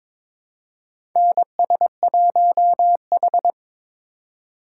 Morse code, heard as a keyed tone, about 22 wpm, 700 Hz. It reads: NS1H
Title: USA Call Signs - 4 Characters - 2x Character Spacing - 22wpm